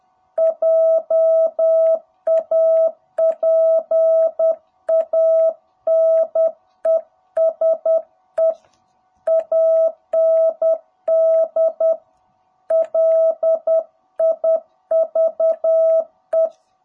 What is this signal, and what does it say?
Morse code, heard as a keyed tone, about 10 wpm, 660 Hz. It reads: JAPANESE AND LIVE
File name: jh8cbh / 2021cw